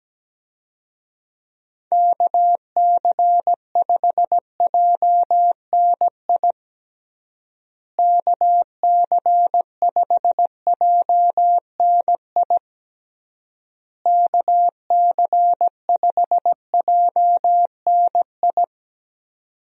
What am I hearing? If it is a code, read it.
Morse code, heard as a keyed tone, about 17 wpm, 700 Hz. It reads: KC5JNI KC5JNI KC5JNI
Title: USA Call Signs - 6 Characters - Repeated 3x - 17wpm